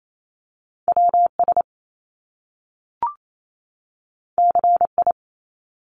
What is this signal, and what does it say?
Morse code, heard as a keyed tone, about 28 wpm, 700 Hz. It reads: WS CI
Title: Top Two Letter Combinations in Words 28wpm